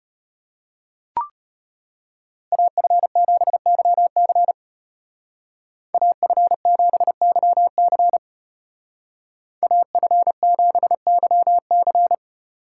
Morse code, heard as a keyed tone, about 34 wpm, 700 Hz. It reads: AF7YC AF7YC AF7YC